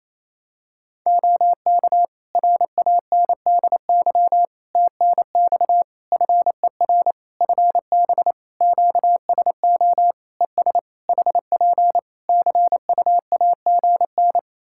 Morse code, heard as a keyed tone, about 27 wpm, 700 Hz. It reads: OK RANDY TNX FER FB QSO ES HP CUAGN
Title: Practice QSO - Exchange 4 - Ending - 28wpm